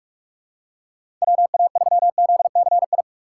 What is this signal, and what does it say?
Morse code, heard as a keyed tone, about 42 wpm, 700 Hz. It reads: WA3ZCI
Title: USA Call Signs - 6 Characters - 2x Character Spacing - 45wpm